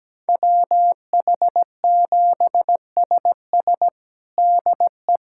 Morse code, heard as a keyed tone, about 17 wpm, 700 Hz. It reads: WH7SS DE
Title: USA Callsign DE Callsign - 5 Characters - 17wpm